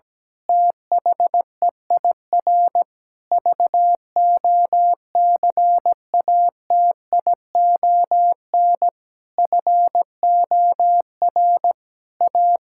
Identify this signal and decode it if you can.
Morse code, heard as a keyed tone, about 17 wpm, 700 Hz. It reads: THEIR VOCATION FOR A
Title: Pangram Challenge v2 - 17wpm